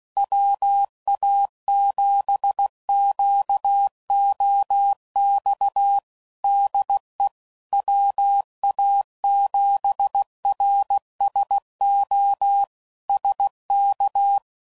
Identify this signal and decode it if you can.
Morse code, heard as a keyed tone, about 16 wpm, 800 Hz. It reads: WA7QOX DE WA7RSO SK
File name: CW Code Groups @15WPM / QSOs / WA7QOX QRX